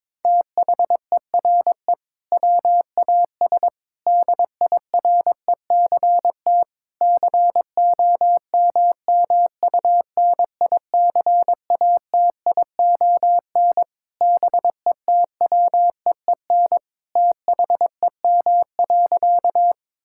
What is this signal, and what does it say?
Morse code, heard as a keyed tone, about 22 wpm, 700 Hz. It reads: THERE WAS DIRECT COMMUNICATION BETWEEN THEM.
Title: Sentences from Top 1000 Words - Review Part 6 - 22wpm